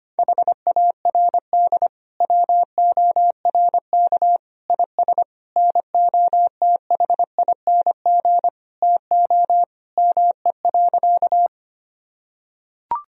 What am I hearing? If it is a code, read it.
Morse code, heard as a keyed tone, about 25 wpm, 700 Hz. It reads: HARD WORK IS NOTHING TO ME.
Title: Sentences from Top 1000 Words - Review Part 1 - 25wpm